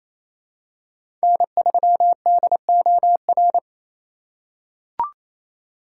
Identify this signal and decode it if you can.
Morse code, heard as a keyed tone, about 28 wpm, 700 Hz. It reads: N3DOR
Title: USA Call Signs - 5 Characters - 4x Character Spacing - 28wpm